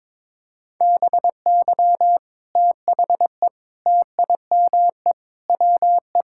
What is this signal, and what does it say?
Morse code, heard as a keyed tone, about 22 wpm, 700 Hz. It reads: BY THE TIME WE
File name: practice / 22wpm / Sentences with 3-Word Phrases - 22wpm